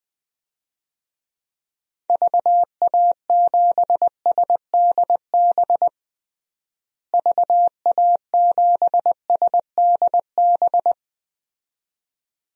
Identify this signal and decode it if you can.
Morse code, heard as a keyed tone, about 20 wpm, 700 Hz. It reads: VA7SDB VA7SDB
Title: DXCC-100 Call Signs - 6 Characters - Repeated 2x - 20wpm